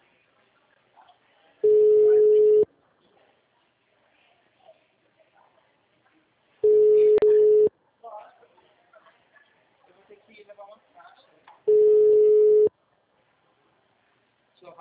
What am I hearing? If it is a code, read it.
Ringback tone (Europe, 425 Hz), with speech.